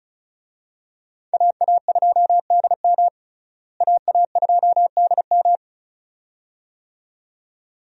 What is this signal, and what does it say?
Morse code, heard as a keyed tone, about 34 wpm, 700 Hz. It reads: AA2DM AA2DM
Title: USA Call Signs - 5 Characters - Repeated 2x - 35wpm